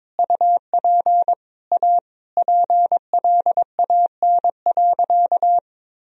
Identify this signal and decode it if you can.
Morse code, heard as a keyed tone, about 22 wpm, 700 Hz. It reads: UP A PLAN.
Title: Sentences from Top 1000 Words - Review Part 2 - 22wpm